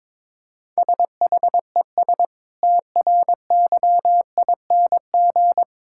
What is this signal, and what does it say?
Morse code, heard as a keyed tone, about 22 wpm, 700 Hz. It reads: SHES TRYING
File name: practice / 22wpm / Sentences with 2-Word Phrases - 22wpm